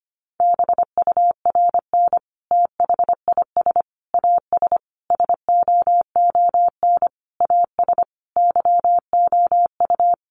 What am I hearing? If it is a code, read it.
Morse code, heard as a keyed tone, about 25 wpm, 700 Hz. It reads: BURN THIS AS SOON AS YOU
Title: Sentences from Top 1300 Words 25wpm